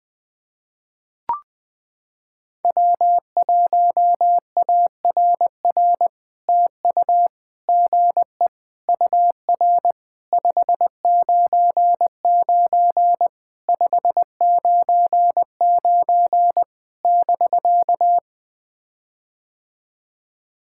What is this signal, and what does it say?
Morse code, heard as a keyed tone, about 20 wpm, 700 Hz. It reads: W1ARR TU GE UR 599 599 <BK>
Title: POTA QSO Practice - 20wpm